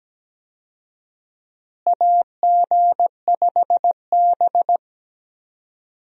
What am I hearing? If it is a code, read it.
Morse code, heard as a keyed tone, about 17 wpm, 700 Hz. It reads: AG5B
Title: USA Call Signs - 4 Characters - Repeated 2x - 17wpm